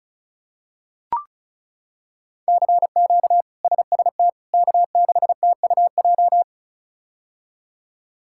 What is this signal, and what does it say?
Morse code, heard as a keyed tone, about 36 wpm, 700 Hz. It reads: CQ SST K6TUJ
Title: K1USN SST QSO Practice - 35wpm